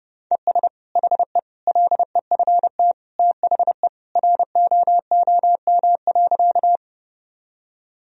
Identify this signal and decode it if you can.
Morse code, heard as a keyed tone, about 30 wpm, 700 Hz. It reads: ES HE LEFT THE ROOM.
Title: Sentences from Top 300 Words and Abbreviations 30wpm